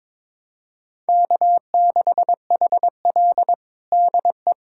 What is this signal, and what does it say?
Morse code, heard as a keyed tone, about 22 wpm, 700 Hz. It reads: K6HL DE
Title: USA Callsign DE Callsign - 4 Characters - 22wpm